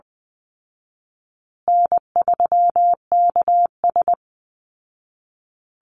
Morse code, heard as a keyed tone, about 20 wpm, 700 Hz. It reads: N3KS
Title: Easy Call Signs Repeated 3x - 20wpm